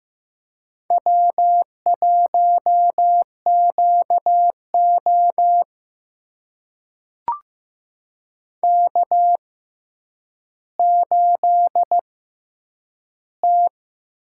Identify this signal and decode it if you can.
Morse code, heard as a keyed tone, about 15 wpm, spaced as for 9 wpm, 700 Hz. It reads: W1QO K8T
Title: USA Call Signs - 4 Characters - 6x Character Spacing - 15wpm